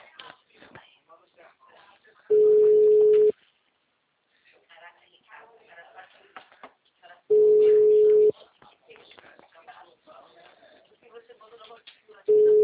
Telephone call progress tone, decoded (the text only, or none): ringback tone